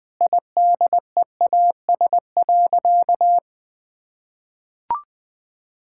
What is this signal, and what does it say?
Morse code, heard as a keyed tone, about 20 wpm, 700 Hz. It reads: IDEAS.
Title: Sentences with 4-Word Phrases - Spoken Beginning - 20wpm